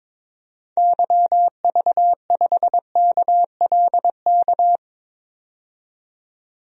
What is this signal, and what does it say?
Morse code, heard as a keyed tone, about 22 wpm, 700 Hz. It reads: YV5KLK
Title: DXCC-100 Call Signs - 6 Characters - 22wpm